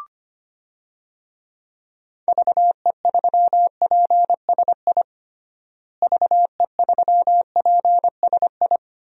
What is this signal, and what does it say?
Morse code, heard as a keyed tone, about 25 wpm, 700 Hz. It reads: VE3PSI VE3PSI
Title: DXCC-100 Call Signs - 6 Characters - Repeated 2x - 25wpm